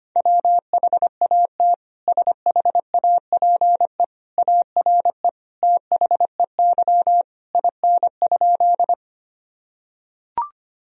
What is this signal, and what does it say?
Morse code, heard as a keyed tone, about 25 wpm, 700 Hz. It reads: WHAT SHAPE ARE THEY IN?
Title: Sentences from Top 1000 Words - Review Part 3 - 25wpm